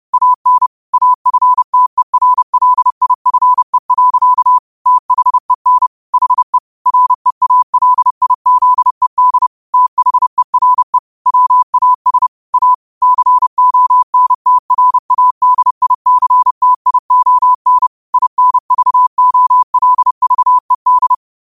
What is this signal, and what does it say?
Morse code, heard as a keyed tone, about 30 wpm, 1000 Hz. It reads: AN AFTERLIFE. THEN HE REALIZED THERE WAS A CONTRADICTION INVOLVED
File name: qotdfiles / 30wpm